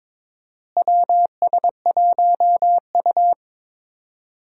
Morse code, heard as a keyed tone, about 22 wpm, 700 Hz. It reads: WS1U
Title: USA Call Signs - 4 Characters - Repeated 2x - 22wpm